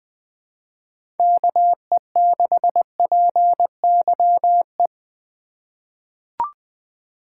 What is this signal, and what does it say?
Morse code, heard as a keyed tone, about 20 wpm, 700 Hz. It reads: KE6PYE
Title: USA Call Signs - 2x Character Spacing - 20wpm